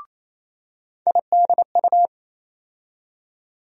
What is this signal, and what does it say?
Morse code, heard as a keyed tone, about 28 wpm, 700 Hz. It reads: IDU
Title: USA Call Sign Postfixes - 3 Letters 28wpm